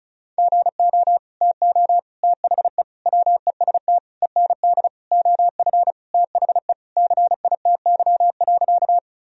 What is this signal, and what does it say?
Morse code, heard as a keyed tone, about 36 wpm, 700 Hz. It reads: GO TO THE WEST END OF THE CITY.
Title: Sentences from Top 1000 Words - Review Part 2 - 35wpm